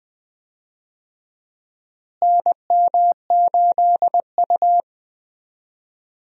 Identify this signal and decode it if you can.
Morse code, heard as a keyed tone, about 21 wpm, 700 Hz. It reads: NM8U
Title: USA Call Signs - 4 Characters - Repeated 2x - 20wpm